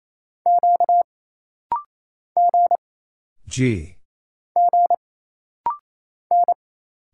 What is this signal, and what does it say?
Morse code, heard as a keyed tone, about 28 wpm, 700 Hz. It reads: Q G G N
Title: Single Letters - Rapid-Fire 28wpm